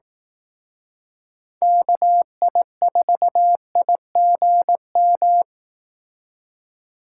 Morse code, heard as a keyed tone, about 18 wpm, 700 Hz. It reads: KI4IGM
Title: Speed Racing - USA 2x3 Call Signs 15wpm